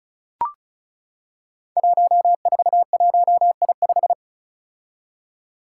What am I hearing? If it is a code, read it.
Morse code, heard as a keyed tone, about 36 wpm, 700 Hz. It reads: 141I5
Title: Five Challenging Letter-Number Combinations 35wpm